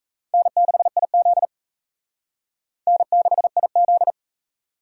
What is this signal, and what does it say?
Morse code, heard as a keyed tone, about 35 wpm, 700 Hz. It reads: N6IZ N6IZ
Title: Speed Racing - USA 1x2 Call Signs 28wpm